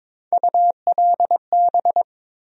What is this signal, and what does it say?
Morse code, heard as a keyed tone, about 22 wpm, 700 Hz. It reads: ULB
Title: USA Call Sign Postfixes - 3 Letters 22wpm